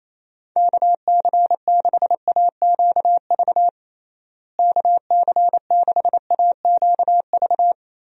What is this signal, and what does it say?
Morse code, heard as a keyed tone, about 28 wpm, 700 Hz. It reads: KC6AQV KC6AQV
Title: USA Call Signs - 6 Characters - Repeated 2x - 28wpm